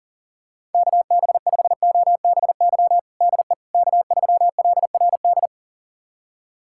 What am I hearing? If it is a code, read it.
Morse code, heard as a keyed tone, about 40 wpm, 700 Hz. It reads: KB5OBY DE K3LRD
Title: USA Callsign DE Callsign 40wpm